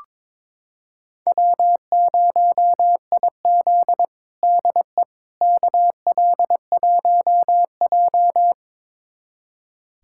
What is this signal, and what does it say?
Morse code, heard as a keyed tone, about 22 wpm, 700 Hz. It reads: W0IZ DE KL1J
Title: USA Callsign DE Callsign - 4 Characters - 22wpm